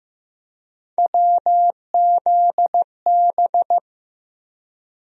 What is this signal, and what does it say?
Morse code, heard as a keyed tone, about 15 wpm, 700 Hz. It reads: WZB
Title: USA Call Sign Postfixes - 3 Letters 15wpm